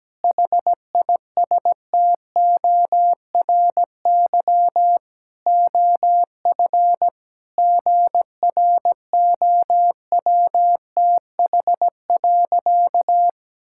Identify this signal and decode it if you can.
Morse code, heard as a keyed tone, about 17 wpm, 700 Hz. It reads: HISTORY OF GROWTH.
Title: Sentences from Top 1000 Words - Review Part 5 - 17wpm